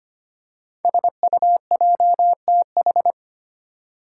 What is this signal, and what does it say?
Morse code, heard as a keyed tone, about 25 wpm, 700 Hz. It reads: SUJTH
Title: Code Groups - 5 Letter:Numbers - 25wpm